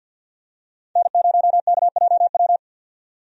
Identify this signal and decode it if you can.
Morse code, heard as a keyed tone, about 50 wpm, 700 Hz. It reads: N0KJW
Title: USA Call Signs - 5 Characters - 50wpm